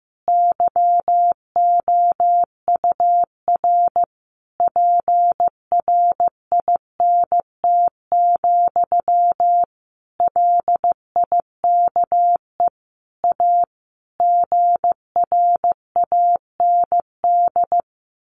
Morse code, heard as a keyed tone, about 15 wpm, 700 Hz. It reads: YOUR PRINT, LIKE A GRAND
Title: A Sound Of Thunder 15wpm